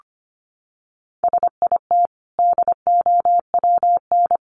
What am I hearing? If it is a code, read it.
Morse code, heard as a keyed tone, about 25 wpm, 700 Hz. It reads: SIT DOWN
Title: Sentences from Top 1100 Words 25wpm